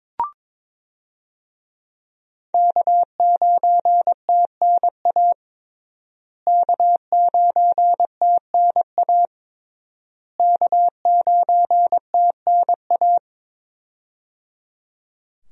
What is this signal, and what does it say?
Morse code, heard as a keyed tone, about 22 wpm, 700 Hz. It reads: K9TNA K9TNA K9TNA
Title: USA Call Signs - 5 Characters - Repeated 3x - 22wpm